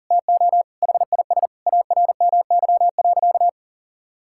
Morse code, heard as a keyed tone, about 37 wpm, 700 Hz. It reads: TO HIS ARMY.
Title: Sentences from Top 1000 Words - Review Part 6 - 40wpm